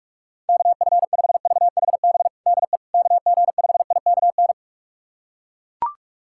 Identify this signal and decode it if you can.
Morse code, heard as a keyed tone, about 41 wpm, 700 Hz. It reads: KF5VHB DE KG5IKN
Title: USA Callsign DE Callsign - 6 Characters - 45wpm